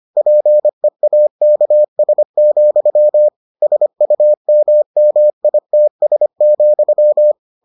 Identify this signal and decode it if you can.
Morse code, heard as a keyed tone, about 25 wpm, 600 Hz. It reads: PEAKS, SUMMITS,